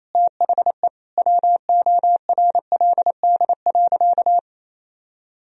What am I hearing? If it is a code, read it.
Morse code, heard as a keyed tone, about 28 wpm, 700 Hz. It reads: THE WORLD.
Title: Sentences from Top 1000 Words - Review Part 4 - 28wpm